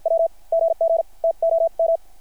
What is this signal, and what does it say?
Morse code, heard as a keyed tone, about 53 wpm, spaced as for 44 wpm, 690 Hz. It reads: W GG TOM
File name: archived / CRSnet